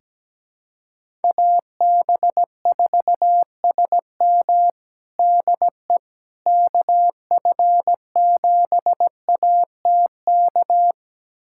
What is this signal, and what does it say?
Morse code, heard as a keyed tone, about 17 wpm, 700 Hz. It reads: AB4SM DE KF7ATK
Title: USA Callsign DE Callsign 17wpm